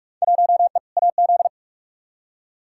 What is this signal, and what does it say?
Morse code, heard as a keyed tone, about 42 wpm, 700 Hz. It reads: 1E AZ
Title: Field Day Exchanges - 45wpm